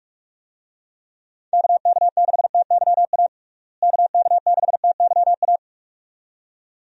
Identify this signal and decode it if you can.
Morse code, heard as a keyed tone, about 42 wpm, 700 Hz. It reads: KK6TYA KK6TYA
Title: USA Call Signs - 6 Characters - Repeated 2x - 45wpm